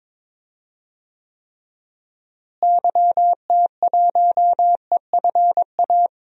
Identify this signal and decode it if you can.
Morse code, heard as a keyed tone, about 22 wpm, 700 Hz. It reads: YT1EFA
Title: DXCC-100 Call Signs - 6 Characters - Repeated 2x - 22wpm